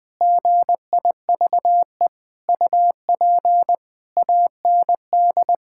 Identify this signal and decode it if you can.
Morse code, heard as a keyed tone, about 19 wpm, 700 Hz. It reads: GIVE UP AND
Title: Sentences from Top 1000 Words - Review Part 6 - 20wpm